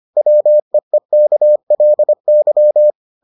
Morse code, heard as a keyed tone, about 25 wpm, 600 Hz. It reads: WEEKLY